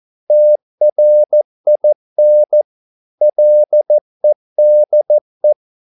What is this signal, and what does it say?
Morse code, heard as a keyed tone, about 14 wpm, 600 Hz. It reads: TRIN LEDE
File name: Korte ord DK 14wpm